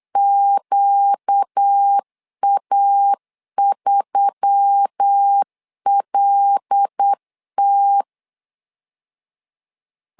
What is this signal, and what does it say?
Morse code, heard as a keyed tone, about 8 wpm, 800 Hz. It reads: QA3LT
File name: LEZIONI / 5 LBVC / 5 L 8wpm 2831